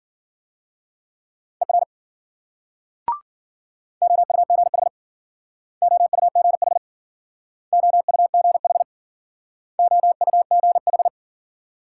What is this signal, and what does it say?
Morse code, heard as a keyed tone, about 54 wpm, 700 Hz. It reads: ER OUGH OUGH OUGH OUGH